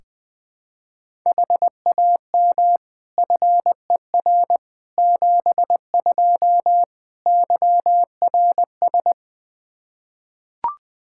Morse code, heard as a keyed tone, about 21 wpm, 700 Hz. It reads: HAM FER 72 YRS
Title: Practice QSO - Exchange 3 - More Info - 20wpm